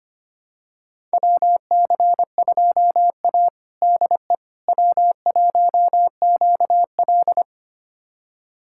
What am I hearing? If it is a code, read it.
Morse code, heard as a keyed tone, about 25 wpm, 700 Hz. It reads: WC2A DE W1QL
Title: USA Callsign DE Callsign - 4 Characters - 25wpm